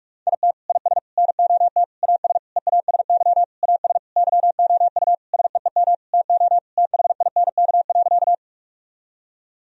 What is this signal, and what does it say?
Morse code, heard as a keyed tone, about 46 wpm, 700 Hz. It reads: IT IS NOT AS EASY AS YOU SEEM TO THINK.
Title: Sentences from Top 1000 Words - Review Part 1 - 45wpm